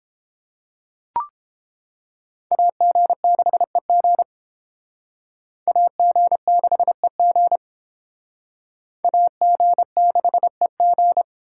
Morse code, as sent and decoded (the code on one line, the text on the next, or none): .- --. -.... --. / .- --. -.... --. / .- --. -.... --.
AG6EG AG6EG AG6EG